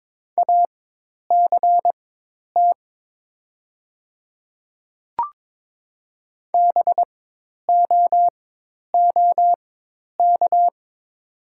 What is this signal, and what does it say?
Morse code, heard as a keyed tone, about 22 wpm, spaced as for 10 wpm, 700 Hz. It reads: ACT BOOK
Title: 4x Character Spacing - Top 300 Words - 22wpm